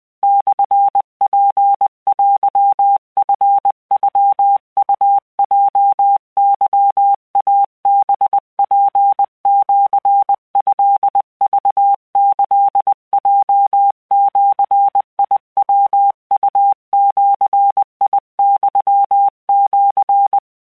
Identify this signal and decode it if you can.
Morse code (Japanese Wabun code), heard as a keyed tone, about 20 wpm, 800 Hz. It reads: モツテチノウヲケイハツシトクキヲジヤウジユシ
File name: keuikuchiyokugo20wptan